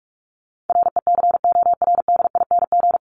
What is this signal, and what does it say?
Morse code, heard as a keyed tone, about 44 wpm, 700 Hz. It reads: RECORDING
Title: Top 1501-1750 Words 45wpm